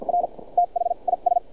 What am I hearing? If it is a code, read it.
Morse code, heard as a keyed tone, about 54 wpm, spaced as for 40 wpm, 700 Hz. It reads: R THIS
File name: archived / CRSnet